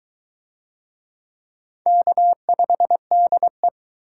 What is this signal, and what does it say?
Morse code, heard as a keyed tone, about 20 wpm, 700 Hz. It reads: K5DE